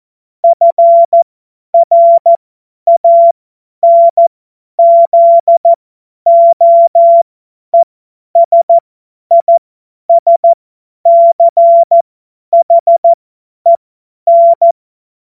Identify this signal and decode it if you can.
Morse code, heard as a keyed tone, about 14 wpm, spaced as for 10 wpm, 680 Hz. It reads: FRANZOESISCHEN
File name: Morsekurs / Witze / Witz 11. Januar 2022, 50 F70